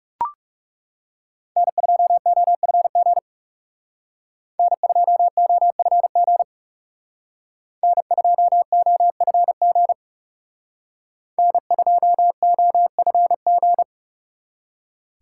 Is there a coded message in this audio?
Yes: Morse code from 1.6 to 13.8 s, heard as a keyed tone, about 38 wpm, 700 Hz. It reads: N2OFG N2OFG N2OFG N2OFG